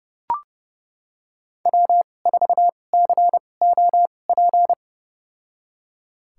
Morse code, heard as a keyed tone, about 30 wpm, spaced as for 22 wpm, 700 Hz. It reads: W4COP
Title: USA Call Signs - 5 Characters - 2x Character Spacing - 30wpm